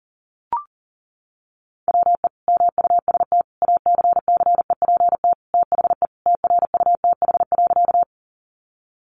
Morse code, heard as a keyed tone, about 40 wpm, 700 Hz. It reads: WE MUST ACCEPT THE TRUTH.
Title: Sentences from Top 1300 Words 40wpm